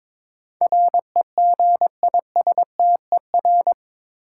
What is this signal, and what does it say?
Morse code, heard as a keyed tone, about 22 wpm, 700 Hz. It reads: REGISTER
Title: Sentences from Top 1200 Words - New Word Drill - 22wpm